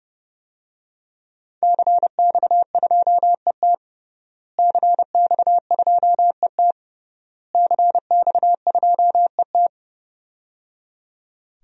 Morse code, heard as a keyed tone, about 30 wpm, 700 Hz. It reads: CX2ET CX2ET CX2ET
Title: DXCC-100 Call Signs - 5 Characters - Repeated 3x - 30wpm